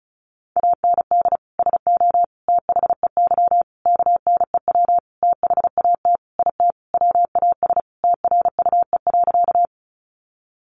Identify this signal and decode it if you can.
Morse code, heard as a keyed tone, about 34 wpm, 700 Hz. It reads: AND SO THEY KNEW THAT IT WAS TRUE.
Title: Sentences from Top 400 Words 35wpm